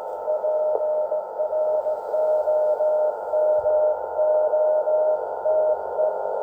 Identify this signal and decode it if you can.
Morse code, heard as a keyed tone, about 17 wpm, 620 Hz, with noise. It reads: PROMOTE